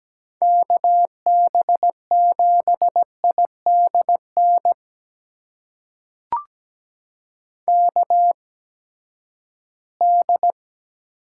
Morse code, heard as a keyed tone, about 17 wpm, spaced as for 12 wpm, 700 Hz. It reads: KB7IDN KD
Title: USA Call Signs - 6 Characters - 8x Character Spacing - 17wpm